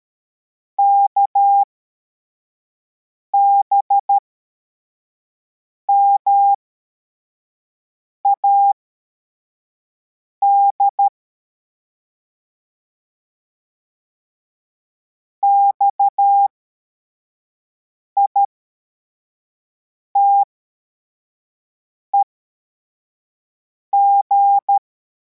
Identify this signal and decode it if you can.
Morse code, heard as a keyed tone, about 13 wpm, spaced as for 4 wpm, 800 Hz. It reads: KBMAD XITEG
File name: formation / cw / Text42